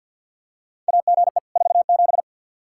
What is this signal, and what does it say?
Morse code, heard as a keyed tone, about 46 wpm, 700 Hz. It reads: AGE 47
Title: Practice QSO - Exchange 3 - More Info - 50wpm